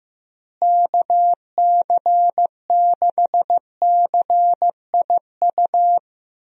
Morse code, heard as a keyed tone, about 15 wpm, 700 Hz. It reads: KC6CIU
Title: USA Call Signs - 6 Characters - Repeated 3x - 15wpm